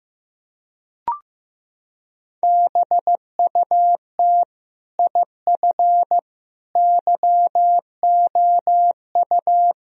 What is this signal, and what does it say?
Morse code, heard as a keyed tone, about 15 wpm, 700 Hz. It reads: BUT IF YOU
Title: Sentences from Top 1000 Words - Review Part 3 - 15wpm